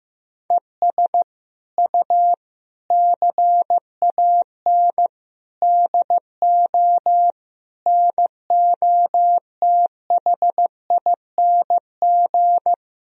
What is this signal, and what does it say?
Morse code, heard as a keyed tone, about 16 wpm, 700 Hz. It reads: ES U CAN DO NOTHING
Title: Sentences from Top 500 Words and Abbreviations 15wpm